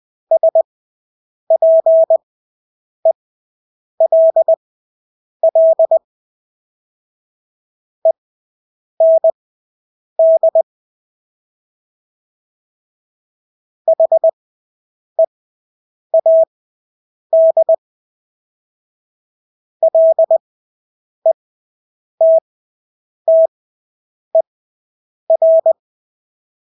Morse code, heard as a keyed tone, about 20 wpm, spaced as for 8 wpm, 650 Hz. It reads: SPELL END HEAD LETTER
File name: Engelske ord 20/8 wpm